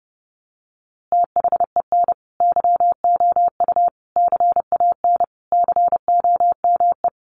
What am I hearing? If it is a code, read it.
Morse code, heard as a keyed tone, about 30 wpm, 700 Hz. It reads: THEN YOU CAN COME
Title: A Sound Of Thunder 30wpm